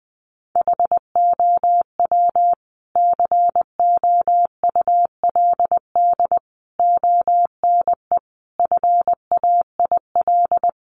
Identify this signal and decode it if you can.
Morse code, heard as a keyed tone, about 20 wpm, 700 Hz. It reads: HOW COULD ONE FAIL
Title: Sentences from Top 1100 Words 20wpm